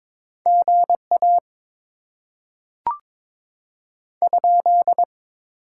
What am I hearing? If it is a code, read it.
Morse code, heard as a keyed tone, about 22 wpm, 700 Hz. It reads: GA ?
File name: Top 90 QSO Elements - 6x Character Spacing - 22wpm